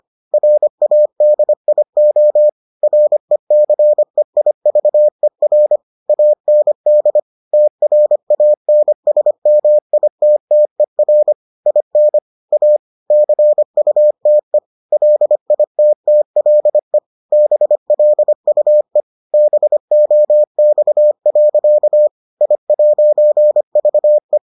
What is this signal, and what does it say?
Morse code, heard as a keyed tone, about 25 wpm, 600 Hz. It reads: RADIO RECEIVER AND TRANSMITTER IN A CUTE LITTLE BLUE BOX. I'VE